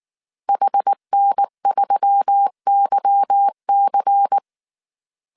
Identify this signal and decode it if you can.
Morse code, heard as a keyed tone, about 19 wpm, 800 Hz. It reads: HN3YC
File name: CW / DET / 1 9 1500C S A 18wpm 2459